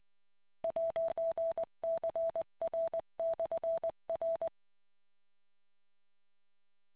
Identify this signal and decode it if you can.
Morse code, heard as a keyed tone, about 20 wpm, 660 Hz. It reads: 'CR/R